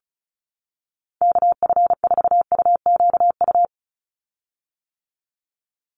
Morse code, heard as a keyed tone, about 34 wpm, 700 Hz. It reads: KF4UQU
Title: USA Call Signs 35wpm